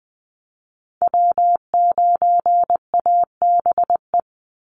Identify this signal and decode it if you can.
Morse code, heard as a keyed tone, about 20 wpm, 700 Hz. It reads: W9ABE
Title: USA Call Signs 20wpm